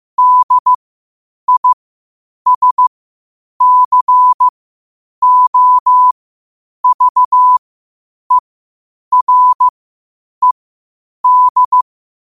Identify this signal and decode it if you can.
Morse code, heard as a keyed tone, about 15 wpm, spaced as for 8 wpm, 1000 Hz. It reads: DISCOVERED
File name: qotdfiles / 8wpm